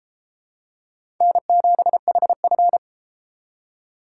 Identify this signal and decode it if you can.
Morse code, heard as a keyed tone, about 33 wpm, 700 Hz. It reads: N7HF